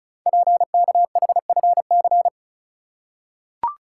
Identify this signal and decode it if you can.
Morse code, heard as a keyed tone, about 36 wpm, 690 Hz. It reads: PKHFC
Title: Code Groups - 5 Letters - 35wpm